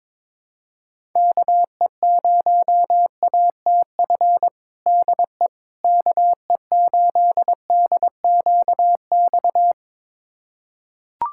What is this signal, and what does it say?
Morse code, heard as a keyed tone, about 22 wpm, 700 Hz. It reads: KE0ATF DE KE8DQX
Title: USA Callsign DE Callsign 22wpm